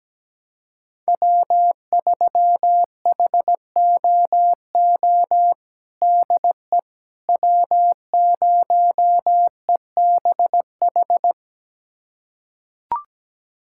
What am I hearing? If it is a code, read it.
Morse code, heard as a keyed tone, about 17 wpm, 700 Hz. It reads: W3HOO DE W0EBH
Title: USA Callsign DE Callsign - 5 Characters - 17wpm